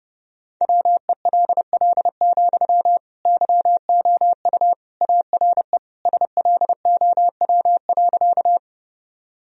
Morse code, heard as a keyed tone, about 30 wpm, 700 Hz. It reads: WELL, YOU ARE SLOW.
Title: Sentences from Top 1000 Words - Review Part 2 - 30wpm